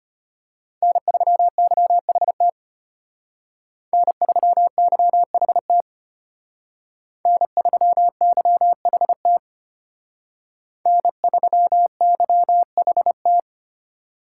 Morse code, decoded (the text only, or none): N3YHT N3YHT N3YHT N3YHT